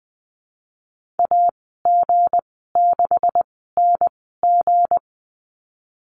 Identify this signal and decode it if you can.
Morse code, heard as a keyed tone, about 20 wpm, spaced as for 15 wpm, 700 Hz. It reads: AG6NG
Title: 161 Lesson 26g - Letter g - Call Signs 20wpm